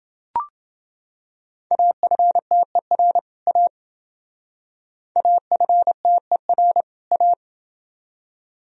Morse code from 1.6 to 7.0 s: AFTER A AFTER

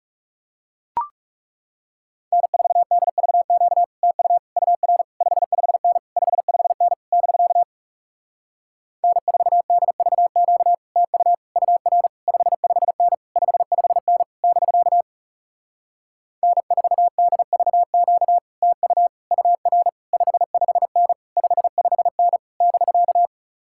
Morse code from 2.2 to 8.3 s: N4DVQ TU UR 55N 55N <BK>

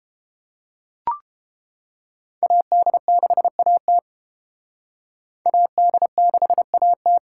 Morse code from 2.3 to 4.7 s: AD6AT

